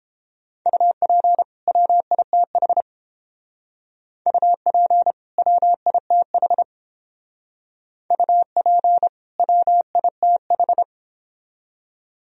Morse code, heard as a keyed tone, about 30 wpm, 700 Hz. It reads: UP WITH UP WITH UP WITH